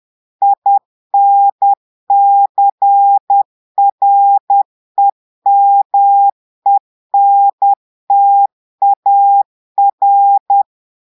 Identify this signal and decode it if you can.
Morse code, heard as a keyed tone, about 10 wpm, 800 Hz. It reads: INCREMENTAR
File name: portugais texte en clair 2 vitesse 10 mots minutes